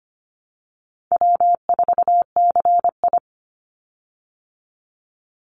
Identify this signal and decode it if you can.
Morse code, heard as a keyed tone, about 25 wpm, 700 Hz. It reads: W4CI
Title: Easy Call Signs Repeated 2x - 25wpm